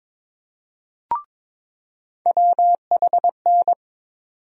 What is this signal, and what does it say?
Morse code, heard as a keyed tone, about 22 wpm, 700 Hz. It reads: WHN
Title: USA Call Sign Postfixes - 3 Letters 22wpm